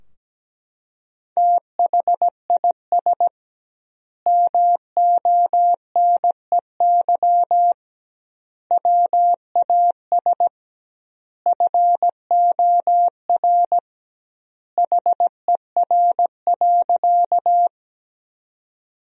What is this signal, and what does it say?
Morse code, heard as a keyed tone, about 17 wpm, 700 Hz. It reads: THIS MONEY WAS FOR HER.
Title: 2x Word-Spacing - Sentences from Top 400 Words - 17wpm